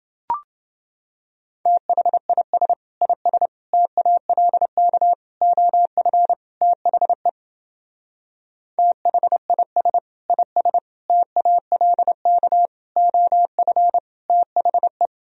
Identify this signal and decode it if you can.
Morse code, heard as a keyed tone, about 28 wpm, 700 Hz. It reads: THIS IS TALK OF THE THIS IS TALK OF THE